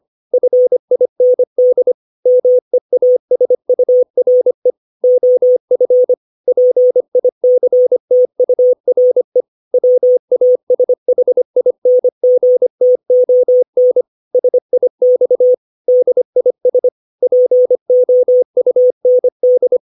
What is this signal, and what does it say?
Morse code, heard as a keyed tone, about 25 wpm, 500 Hz. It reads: FIND MEASURE OF PICTURE WASHINGTON SIX DIS POUND
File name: 632 words / 632 Words 25WPM